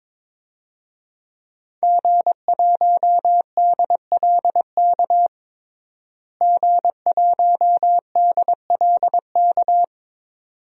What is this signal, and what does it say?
Morse code, heard as a keyed tone, about 22 wpm, 700 Hz. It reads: G1DLK G1DLK
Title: DXCC-100 Call Signs - 5 Characters - Repeated 2x - 22wpm